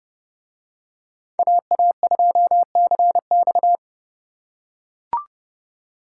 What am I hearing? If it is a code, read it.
Morse code, heard as a keyed tone, about 29 wpm, 700 Hz. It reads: AA2CX